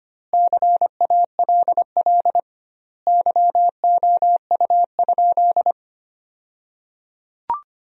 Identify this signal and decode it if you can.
Morse code, heard as a keyed tone, about 25 wpm, 700 Hz. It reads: CALL YOU?
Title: 2x Word-Spacing - Sentences from Top 100 Words - 25wpm